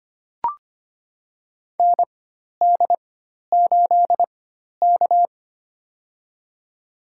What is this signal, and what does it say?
Morse code, heard as a keyed tone, about 25 wpm, spaced as for 12 wpm, 700 Hz. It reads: ND8K
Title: USA Call Signs - 4 Characters - 4x Character Spacing - 25wpm